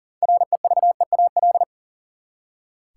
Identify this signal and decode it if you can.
Morse code, heard as a keyed tone, about 40 wpm, 700 Hz. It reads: REVEAL
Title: Six Letter Words - 40wpm